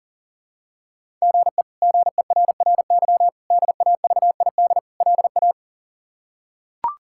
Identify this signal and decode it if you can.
Morse code, heard as a keyed tone, about 40 wpm, 700 Hz. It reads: GE GERRY DAVID LA
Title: K1USN SST QSO Practice - 40wpm